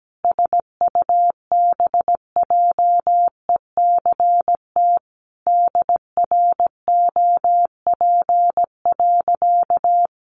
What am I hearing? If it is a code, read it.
Morse code, heard as a keyed tone, about 17 wpm, 700 Hz. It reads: SUBJECT DROP.
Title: Sentences from Top 1400 Words 17wpm